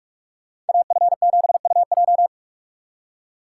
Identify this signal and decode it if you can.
Morse code, heard as a keyed tone, about 58 wpm, 700 Hz. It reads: AF7UJ